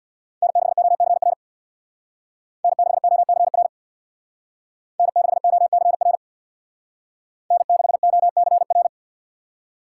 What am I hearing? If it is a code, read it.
Morse code, heard as a keyed tone, about 63 wpm, 700 Hz. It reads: N6OCR N6OCR N6OCR N6OCR